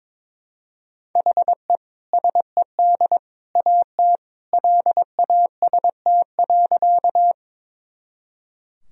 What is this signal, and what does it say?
Morse code, heard as a keyed tone, about 21 wpm, 700 Hz. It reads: HE SED AT LAST.
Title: Sentences from Top 300 Words and Abbreviations 22wpm